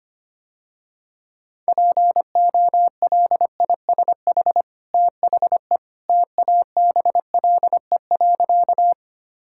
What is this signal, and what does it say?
Morse code, heard as a keyed tone, about 25 wpm, 700 Hz. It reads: POLISH THE TABLE.
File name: Easy Sentences with Hints - 25wpm